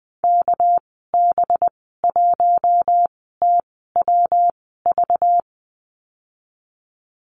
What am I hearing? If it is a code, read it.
Morse code, heard as a keyed tone, about 20 wpm, spaced as for 15 wpm, 700 Hz. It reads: KB1TWV
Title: 200 Lesson 32g - Letter k - Call Signs 20wpm